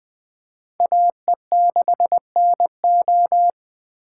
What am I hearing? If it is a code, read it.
Morse code, heard as a keyed tone, about 24 wpm, 700 Hz. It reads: AE6NO
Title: Speed Racing - USA 2x2 Call Signs 20wpm